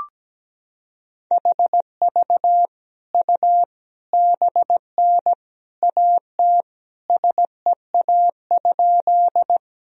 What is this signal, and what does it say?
Morse code, heard as a keyed tone, about 17 wpm, 700 Hz. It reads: HV U BN AT SEA?
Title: Sentences from Top 300 Words and Abbreviations 17wpm